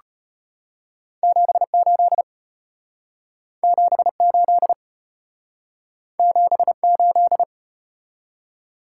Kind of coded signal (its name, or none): Morse code